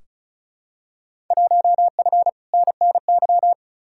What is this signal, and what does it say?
Morse code, heard as a keyed tone, about 36 wpm, 700 Hz. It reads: 1F NNY
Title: Field Day Exchanges - 35wpm